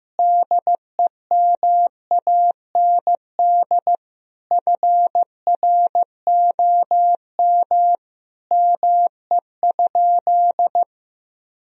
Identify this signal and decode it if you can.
Morse code, heard as a keyed tone, about 15 wpm, 700 Hz. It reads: DEMAND FROM ME?
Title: Sentences from Top 1000 Words - Review Part 4 - 15wpm